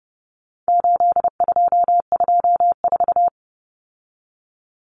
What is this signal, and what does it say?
Morse code, heard as a keyed tone, about 30 wpm, 700 Hz. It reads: 8224
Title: Common Cut Numbers - 4-Digit 30wpm